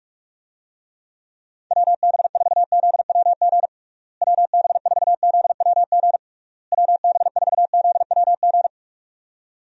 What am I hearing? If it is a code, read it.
Morse code, heard as a keyed tone, about 41 wpm, 700 Hz. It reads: WB4ZWG WB4ZWG WB4ZWG
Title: USA Call Signs - 6 Characters - Repeated 3x - 45wpm